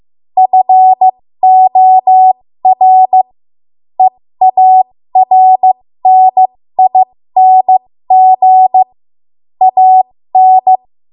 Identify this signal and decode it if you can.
Morse code, heard as a keyed tone, about 15 wpm, spaced as for 13 wpm, 750 Hz. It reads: FOR EARNING AN
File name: Morse / 13WPM